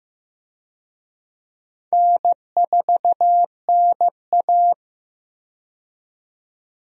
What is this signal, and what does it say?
Morse code, heard as a keyed tone, about 15 wpm, 700 Hz. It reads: N4NA
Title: USA Call Signs - 4 Characters - 15wpm